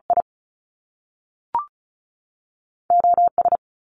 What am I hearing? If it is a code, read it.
Morse code, heard as a keyed tone, about 36 wpm, 700 Hz. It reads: I O S
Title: Top Two Letter Combinations in Words 35wpm